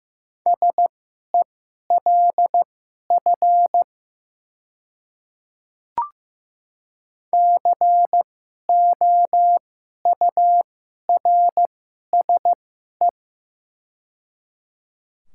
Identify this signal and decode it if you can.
Morse code, heard as a keyed tone, about 15 wpm, spaced as for 11 wpm, 700 Hz. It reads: SELF COURSE
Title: 2x Character Spacing - Top 500 Words - 15wpm